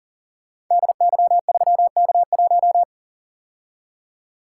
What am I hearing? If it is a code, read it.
Morse code, heard as a keyed tone, about 41 wpm, 700 Hz. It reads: DY3K1
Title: Code Groups - 5 Letter:Numbers - 40wpm